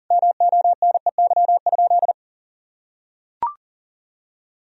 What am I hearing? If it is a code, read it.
Morse code, heard as a keyed tone, about 37 wpm, 700 Hz. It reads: MONEY?
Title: Sentences from Top 1000 Words - Review Part 4 - 40wpm